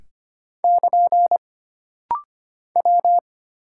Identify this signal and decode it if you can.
Morse code, heard as a keyed tone, about 25 wpm, 700 Hz. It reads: <KN> W
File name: Single Character (Letter:Number:Punctuation:Prosign) - Rapid-Fire - 25wpm